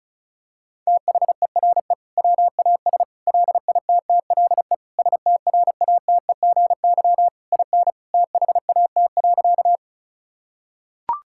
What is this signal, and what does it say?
Morse code, heard as a keyed tone, about 36 wpm, 700 Hz. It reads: THERE WAS LITTLE STRATEGY IN THAT.
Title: Sentences from Top 1000 Words - Review Part 5 - 35wpm